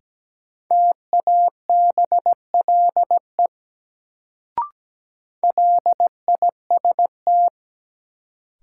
Morse code, heard as a keyed tone, about 17 wpm, 700 Hz. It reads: TABLE LIST
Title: Sentences from Top 400 Words - New Word Drill - 17wpm